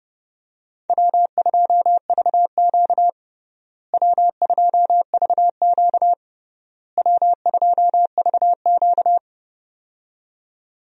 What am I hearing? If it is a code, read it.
Morse code, heard as a keyed tone, about 30 wpm, 700 Hz. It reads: W2VQ W2VQ W2VQ
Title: USA Call Signs - 4 Characters - Repeated 3x - 30wpm